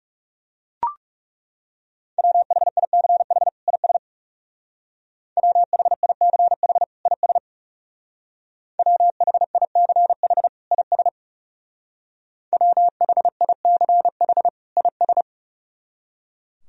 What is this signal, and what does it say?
Morse code, heard as a keyed tone, about 37 wpm, 700 Hz. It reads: WHICH IS WHICH IS WHICH IS WHICH IS
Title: Speed-Racing - Sets of 2 Words 30wpm